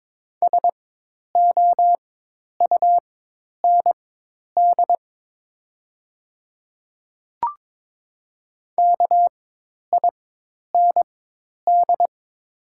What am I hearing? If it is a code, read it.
Morse code, heard as a keyed tone, about 22 wpm, spaced as for 10 wpm, 690 Hz. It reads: SOUND KIND
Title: 4x Character Spacing - Top 300 Words - 22wpm